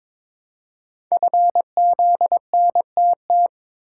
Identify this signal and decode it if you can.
Morse code, heard as a keyed tone, about 22 wpm, 690 Hz. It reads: FZNTT
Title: Code Groups - 5 Letters - 22wpm